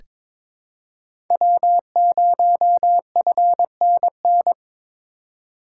Morse code, heard as a keyed tone, about 22 wpm, 700 Hz. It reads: W0FNN
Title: USA Call Signs - 5 Characters - 3x Character Spacing - 22wpm